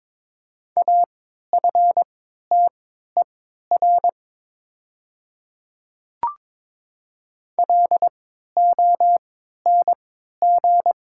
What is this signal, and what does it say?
Morse code, heard as a keyed tone, about 22 wpm, spaced as for 12 wpm, 700 Hz. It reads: AFTER LONG
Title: 3x Character Spacing - Top 200 Words - 22wpm